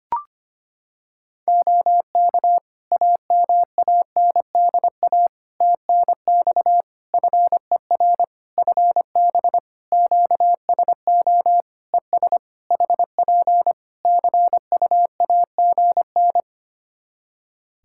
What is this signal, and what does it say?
Morse code, heard as a keyed tone, about 25 wpm, 700 Hz. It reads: OK AMANDA TNX FER FB QSO ES HP CUAGN
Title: Practice QSO - Exchange 4 - Ending - 25wpm